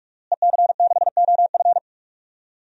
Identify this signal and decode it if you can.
Morse code, heard as a keyed tone, about 45 wpm, 700 Hz. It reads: EC6OF
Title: DXCC-100 Call Signs - 5 Characters - Repeated 2x - 45wpm